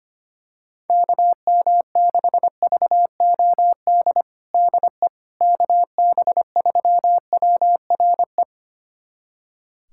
Morse code, heard as a keyed tone, about 25 wpm, 700 Hz. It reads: KM6VOD DE KB3WRE
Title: USA Callsign DE Callsign 25wpm